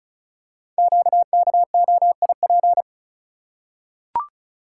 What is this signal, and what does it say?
Morse code, heard as a keyed tone, about 34 wpm, 690 Hz. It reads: QKOIP
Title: Code Groups - 5 Letters - 35wpm